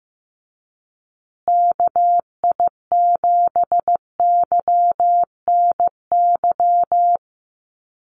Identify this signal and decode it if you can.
Morse code, heard as a keyed tone, about 15 wpm, 700 Hz. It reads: KI7YNY
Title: USA Call Signs Repeated 3x 15wpm